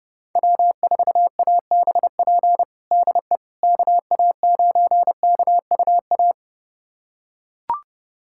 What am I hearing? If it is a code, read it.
Morse code, heard as a keyed tone, about 32 wpm, 700 Hz. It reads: W4ABP DE KA9KUA
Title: USA Callsign DE Callsign 30wpm